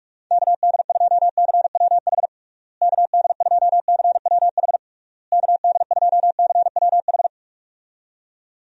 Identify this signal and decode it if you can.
Morse code, heard as a keyed tone, about 41 wpm, 700 Hz. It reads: KD2CWH KD2CWH KD2CWH
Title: USA Call Signs - 6 Characters - Repeated 3x - 45wpm